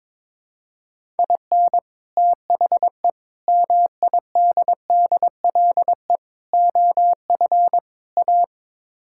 Morse code, heard as a keyed tone, about 22 wpm, 700 Hz. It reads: IN THE MIDDLE OF A
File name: practice / 22wpm / Build A Phrase - 5-Word Phrases - 22wpm